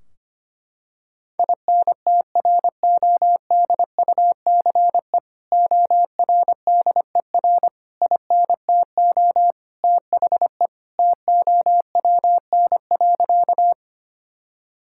Morse code, heard as a keyed tone, about 25 wpm, 700 Hz. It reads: INTRODUCE ORDER INTO THE TOWN.
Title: Sentences from Top 1600 Words 25wpm